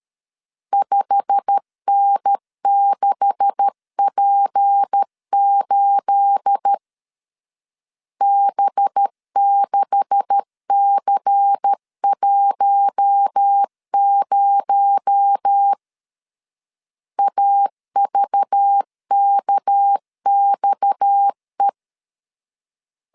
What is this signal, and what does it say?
Morse code, heard as a keyed tone, about 13 wpm, 800 Hz. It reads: 5N6P8 B6C10 AVKXE
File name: CW / DET / 1 9 1500C S A 12wpm 3800